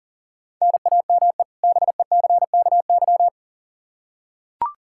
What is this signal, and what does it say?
Morse code, heard as a keyed tone, about 40 wpm, 700 Hz. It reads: NAME BECKY
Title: K1USN SST QSO Practice 40wpm